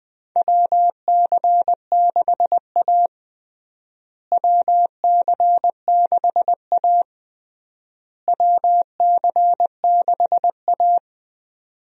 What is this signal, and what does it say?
Morse code, heard as a keyed tone, about 20 wpm, 700 Hz. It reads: WC6A WC6A WC6A
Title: USA Call Signs - 4 Characters - Repeated 3x - 20wpm